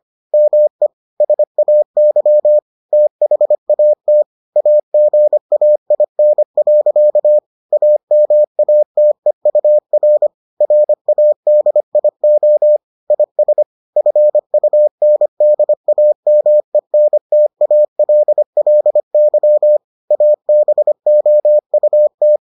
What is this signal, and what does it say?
Morse code, heard as a keyed tone, about 25 wpm, 600 Hz. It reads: ME SAY THAT AGAIN. AMATEUR RADIO IS FUNDAMENTALLY ABOUT